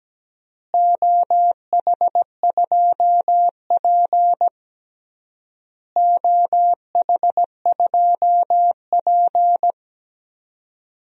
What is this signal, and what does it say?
Morse code, heard as a keyed tone, about 17 wpm, 700 Hz. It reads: OH2P OH2P
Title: DXCC-100 Call Signs - 4 Characters - Repeated 3x - 17wpm